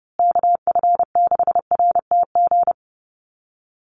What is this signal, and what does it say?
Morse code, heard as a keyed tone, about 30 wpm, 700 Hz. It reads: KF6RTG
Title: USA Call Signs Repeated 3x 30wpm